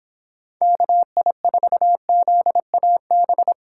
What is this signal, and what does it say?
Morse code, heard as a keyed tone, about 26 wpm, 700 Hz. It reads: KI4ZAB
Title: Speed Racing - USA 2x3 Call Signs 22wpm